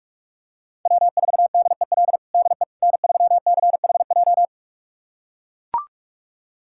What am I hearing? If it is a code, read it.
Morse code, heard as a keyed tone, about 43 wpm, 700 Hz. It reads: W4DEL DE N3CHJ
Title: USA Callsign DE Callsign - 5 Characters - 45wpm